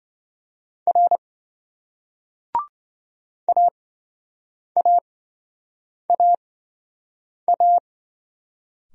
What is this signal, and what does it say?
Morse code, heard as a keyed tone, about 27 wpm, spaced as for 7 wpm, 700 Hz. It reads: R AAAA